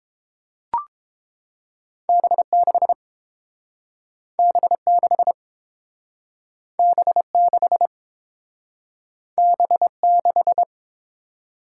Morse code, heard as a keyed tone, about 30 wpm, 700 Hz. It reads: B6 B6 B6 B6